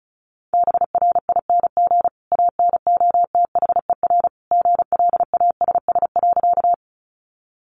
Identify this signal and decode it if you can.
Morse code, heard as a keyed tone, about 34 wpm, 700 Hz. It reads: BRING ANOTHER GLASS.
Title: Sentences from Top 1300 Words 35wpm